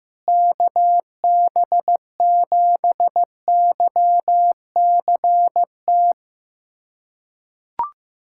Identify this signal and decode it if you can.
Morse code, heard as a keyed tone, about 15 wpm, 700 Hz. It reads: KB7YCT
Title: USA Callsign DE Callsign 15wpm